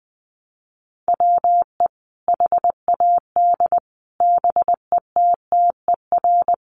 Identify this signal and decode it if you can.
Morse code, heard as a keyed tone, about 20 wpm, 700 Hz. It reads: WE HAD BETTER
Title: Sentences from Top 1400 Words 20wpm